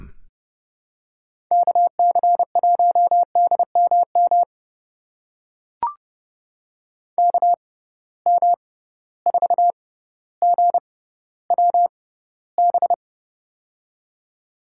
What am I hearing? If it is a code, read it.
Morse code, heard as a keyed tone, about 30 wpm, spaced as for 15 wpm, 700 Hz. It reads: KC1DMM KM4GWB
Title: USA Call Signs - 6x Character Spacing - 30wpm